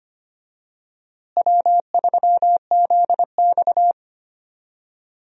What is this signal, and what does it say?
Morse code, heard as a keyed tone, about 25 wpm, 700 Hz. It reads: W3ZX
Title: USA Call Signs - 4 Characters - Repeated 3x - 25wpm